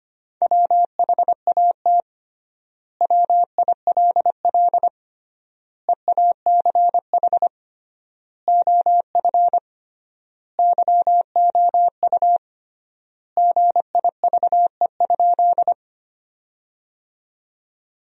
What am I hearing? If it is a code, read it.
Morse code, heard as a keyed tone, about 25 wpm, 700 Hz. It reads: WHAT WILL EACH OF YOU GIVE?
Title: 3x Word-Spacing - Sentences from Top 200 Words - 25wpm